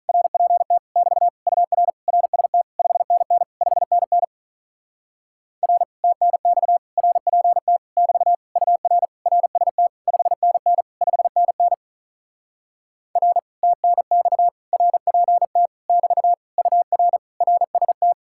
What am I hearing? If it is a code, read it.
Morse code, heard as a keyed tone, about 46 wpm, 700 Hz. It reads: RPT <BT> UR RST 5NN 5NN R TNX RPT <BT> UR RST 5NN 5NN R TNX RPT <BT> UR RST